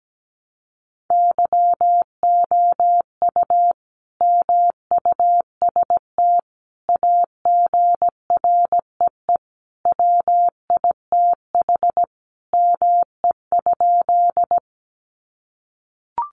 Morse code, heard as a keyed tone, about 17 wpm, 700 Hz. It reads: YOU MUST AGREE WITH ME?
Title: Sentences from Top 1100 Words 17wpm